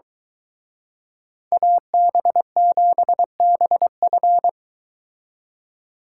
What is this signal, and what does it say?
Morse code, heard as a keyed tone, about 23 wpm, 700 Hz. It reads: AB7BF